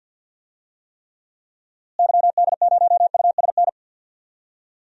Morse code, heard as a keyed tone, about 46 wpm, 700 Hz. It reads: YD0USN
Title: DXCC-100 Call Signs - 6 Characters - Repeated 3x - 50wpm